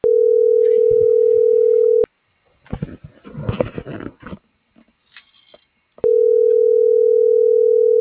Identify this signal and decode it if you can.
Ringback tone (North America), 440 + 480 Hz.